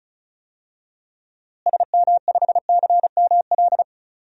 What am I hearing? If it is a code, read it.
Morse code, heard as a keyed tone, about 36 wpm, 700 Hz. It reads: SM5CML
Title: DXCC-100 Call Signs - 6 Characters - Repeated 3x - 35wpm